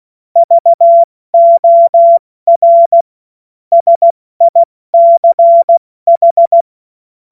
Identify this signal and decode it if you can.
Morse code, heard as a keyed tone, about 16 wpm, 680 Hz. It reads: VOR SICH
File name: Morsekurs / Witze / Witz 06. Dezember 2022 70 F80